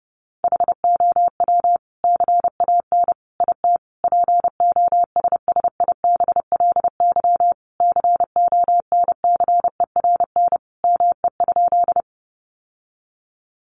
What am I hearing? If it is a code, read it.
Morse code, heard as a keyed tone, about 30 wpm, 700 Hz. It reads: HOW CAN IT POSSIBLY CONCERN ME?
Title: Sentences from Top 1300 Words 30wpm